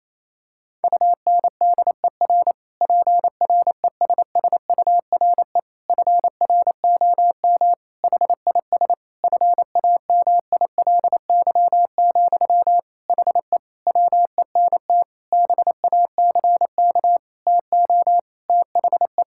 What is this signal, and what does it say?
Morse code, heard as a keyed tone, about 28 wpm, 700 Hz. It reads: UNDER PRESSURE FROM HIS FAMILY, HE WENT BACK TO THE
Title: Sentences from Top 1000 Words - Review Part 6 - 28wpm